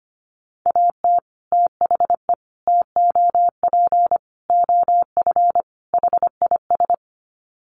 Morse code, heard as a keyed tone, about 25 wpm, 700 Hz. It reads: AT THE TOP OF HIS
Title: Sets of 5 Words Encore 25wpm